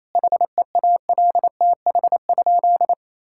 Morse code, heard as a keyed tone, about 27 wpm, 700 Hz. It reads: HEALTH?
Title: Sentences from Top 1000 Words - Review Part 5 - 28wpm